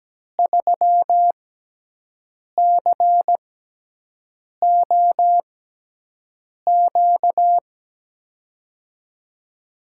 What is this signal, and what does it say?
Morse code, heard as a keyed tone, about 17 wpm, spaced as for 6 wpm, 700 Hz. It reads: 3COQ
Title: USA Call Signs - 6 Characters - 6x Character Spacing - 17wpm